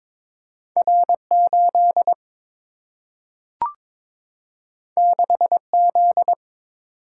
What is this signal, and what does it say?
Morse code, heard as a keyed tone, about 22 wpm, 700 Hz. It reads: R 8 6 Z
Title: Single-Letter Single-Number 22wpm